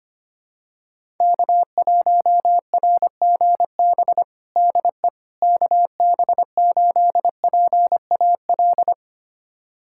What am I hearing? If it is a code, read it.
Morse code, heard as a keyed tone, about 25 wpm, 700 Hz. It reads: K1RGB DE KB8PAL
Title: USA Callsign DE Callsign 25wpm